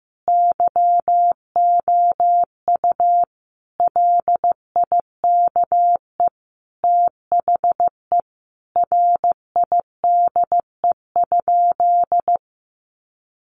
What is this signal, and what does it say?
Morse code, heard as a keyed tone, about 14 wpm, 700 Hz. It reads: YOU LIKE THE RIDE?
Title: Sentences from Top 1200 Words 15wpm